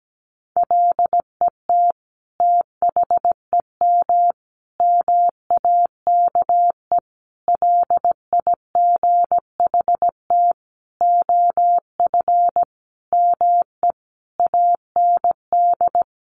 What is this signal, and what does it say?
Morse code, heard as a keyed tone, about 17 wpm, 700 Hz. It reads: LET THEM MAKE LIGHT OF ME AND
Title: Sentences from Top 1300 Words 17wpm